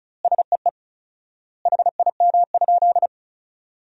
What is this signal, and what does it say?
Morse code, heard as a keyed tone, about 34 wpm, 700 Hz. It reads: SEE HIM?
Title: 4x Word-Spacing - Sentences from Top 100 Words - 35wpm